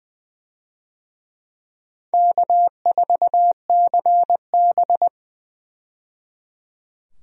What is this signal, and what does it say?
Morse code, heard as a keyed tone, about 20 wpm, 700 Hz. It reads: K4CB
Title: USA Call Signs - 4 Characters - 20wpm